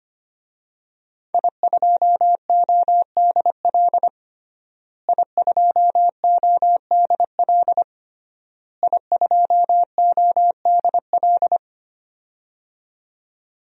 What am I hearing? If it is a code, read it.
Morse code, heard as a keyed tone, about 25 wpm, 700 Hz. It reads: I2ODL I2ODL I2ODL
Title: DXCC-100 Call Signs - 5 Characters - Repeated 3x - 25wpm